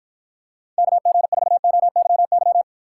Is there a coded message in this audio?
Yes: Morse code from 0.8 to 2.6 s, heard as a keyed tone, about 54 wpm, 700 Hz. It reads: KG4OYY